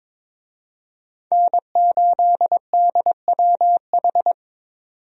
Morse code, heard as a keyed tone, about 22 wpm, 700 Hz. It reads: N8DWH
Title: USA Call Signs - 5 Characters - 22wpm